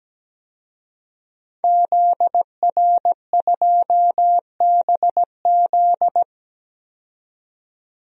Morse code, heard as a keyed tone, about 17 wpm, 700 Hz. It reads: ZR2BZ
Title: DXCC-100 Call Signs - 5 Characters - 17wpm